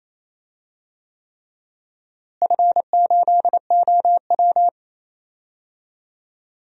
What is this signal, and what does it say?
Morse code, heard as a keyed tone, about 28 wpm, 700 Hz. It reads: F8OW
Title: DXCC-100 Call Signs - 4 Characters - Repeated 3x - 28wpm